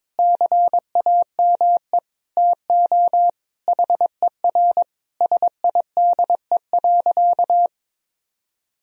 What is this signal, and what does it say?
Morse code, heard as a keyed tone, about 22 wpm, 700 Hz. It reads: CAME TO HER SIDE.
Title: Sentences from Top 1000 Words - Review Part 1 - 22wpm